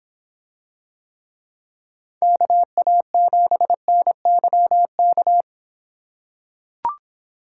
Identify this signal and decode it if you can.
Morse code, heard as a keyed tone, about 25 wpm, 700 Hz. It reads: KA7NYK